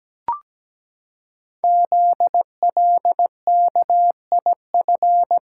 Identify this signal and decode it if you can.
Morse code, heard as a keyed tone, about 17 wpm, 700 Hz. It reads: ZLKIF
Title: Code Groups - 5 Letters - 17wpm